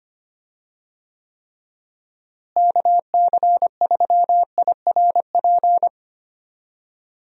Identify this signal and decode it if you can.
Morse code, heard as a keyed tone, about 25 wpm, 700 Hz. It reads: KC3IRP
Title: USA Call Signs - 6 Characters - Repeated 3x - 25wpm